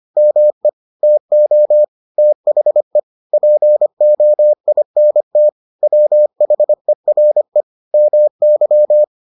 Morse code, heard as a keyed tone, about 25 wpm, 600 Hz. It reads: ME TO THE POINT WHERE MY